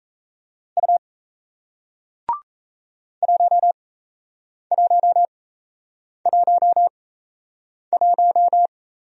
Morse code, heard as a keyed tone, about 38 wpm, spaced as for 8 wpm, 700 Hz. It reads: U 1111